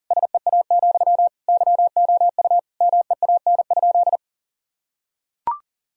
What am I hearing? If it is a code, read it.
Morse code, heard as a keyed tone, about 40 wpm, 700 Hz. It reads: SEA, YOU MEAN?
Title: Sentences from Top 1000 Words - Review Part 1 - 40wpm